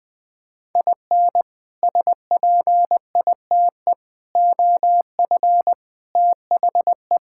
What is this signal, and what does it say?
Morse code, heard as a keyed tone, about 20 wpm, 700 Hz. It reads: IN SPITE OF THE
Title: Sets of 6 Words - 20wpm